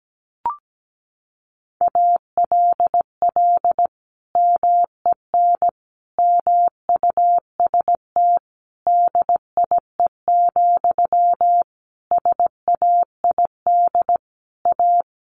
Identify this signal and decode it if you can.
Morse code, heard as a keyed tone, about 17 wpm, 700 Hz. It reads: ALL MEN MUST DIE, SAID A
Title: Sentences from Top 1300 Words 17wpm